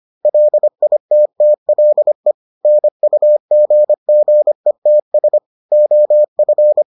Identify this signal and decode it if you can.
Morse code, heard as a keyed tone, about 25 wpm, 600 Hz. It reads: LITTLE NUGGETS OF